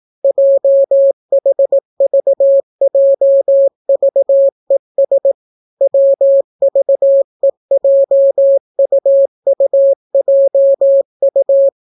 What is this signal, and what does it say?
Morse code, heard as a keyed tone, about 18 wpm, 550 Hz. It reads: JHVJVES WVEJUUJU